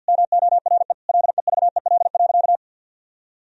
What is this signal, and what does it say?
Morse code, heard as a keyed tone, about 51 wpm, 700 Hz. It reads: MORE LEVEL.
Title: Sentences from Top 1000 Words - Review Part 4 - 50wpm